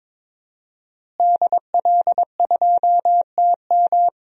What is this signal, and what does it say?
Morse code, heard as a keyed tone, about 22 wpm, 700 Hz. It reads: DL2TM
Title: DXCC-100 Call Signs - 5 Characters - Repeated 3x - 22wpm